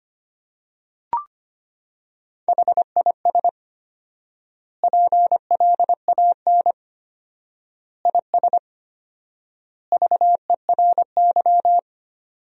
Morse code, heard as a keyed tone, about 25 wpm, 700 Hz. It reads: HIS PLAN IS VERY
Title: 4x Word-Spacing - Sentences from Top 500 Words - 25wpm